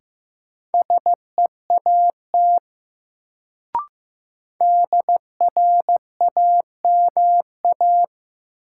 Morse code, heard as a keyed tone, about 16 wpm, 700 Hz. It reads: SEAT DRAMA
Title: Sentences from Top 1500 Words - New Word Drill - 15wpm